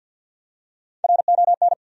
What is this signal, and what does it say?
Morse code, heard as a keyed tone, about 53 wpm, 700 Hz. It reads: RON
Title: USA Common Names - Max 4 Letters - 50wpm